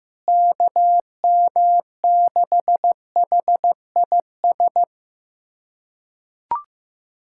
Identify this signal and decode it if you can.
Morse code, heard as a keyed tone, about 15 wpm, 700 Hz. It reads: KM6HIS
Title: USA Callsign DE Callsign - 6 Characters - 15wpm